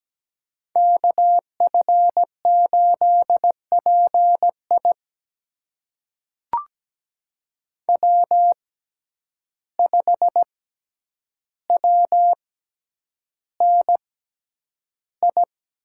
Morse code, heard as a keyed tone, about 17 wpm, spaced as for 9 wpm, 700 Hz. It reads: KF8PI W5WNI
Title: USA Call Signs - 5 Characters - 6x Character Spacing - 17wpm